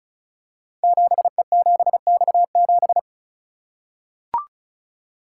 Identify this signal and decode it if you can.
Morse code, heard as a keyed tone, about 36 wpm, 690 Hz. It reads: 7E7X7
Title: Code Groups - 5 Letter:Numbers - 35wpm